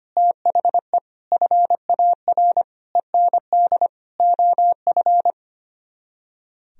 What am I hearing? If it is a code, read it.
Morse code, heard as a keyed tone, about 32 wpm, 700 Hz. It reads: THE FAR END OF